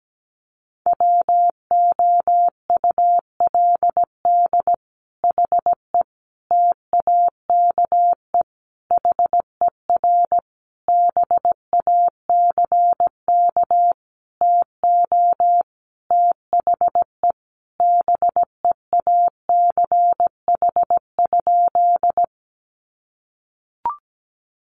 Morse code, heard as a keyed tone, about 17 wpm, 700 Hz. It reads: WOULD HE TAKE HER BACK TO THE BEACH?
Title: Sentences from Top 1500 Words 17wpm